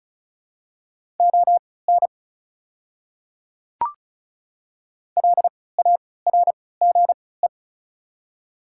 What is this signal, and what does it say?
Morse code, heard as a keyed tone, about 36 wpm, spaced as for 20 wpm, 700 Hz. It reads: ON LARGE
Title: 3x Character Spacing - Top 200 Words - 35wpm